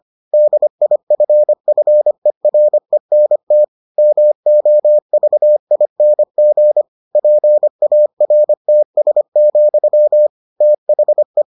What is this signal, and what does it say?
Morse code, heard as a keyed tone, about 25 wpm, 600 Hz. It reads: DIFFERENT MOVING PARTS, THE